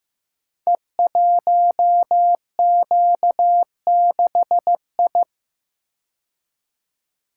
Morse code, heard as a keyed tone, about 15 wpm, 700 Hz. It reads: E1Q6I
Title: Code Groups - 5 Letter:Numbers - 15wpm